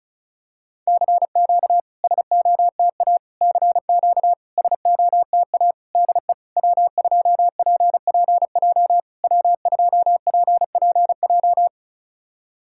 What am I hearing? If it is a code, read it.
Morse code, heard as a keyed tone, about 36 wpm, 700 Hz. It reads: CQ SOTA CQ SOTA DE W2PPJ W2PPJ
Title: SOTA QSO Practice 35wpm